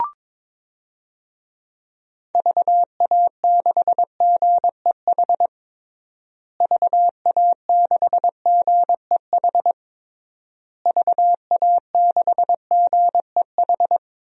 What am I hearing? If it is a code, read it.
Morse code, heard as a keyed tone, about 22 wpm, 700 Hz. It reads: VA6GEH VA6GEH VA6GEH
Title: DXCC-100 Call Signs - 6 Characters - Repeated 3x - 22wpm